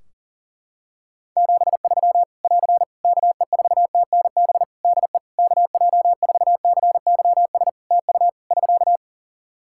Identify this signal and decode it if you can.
Morse code, heard as a keyed tone, about 40 wpm, 700 Hz. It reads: 73 <AR> KE4TNB DE KJ4CYS TU <SK>
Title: Practice QSO - Exchange 4 - Ending - 40wpm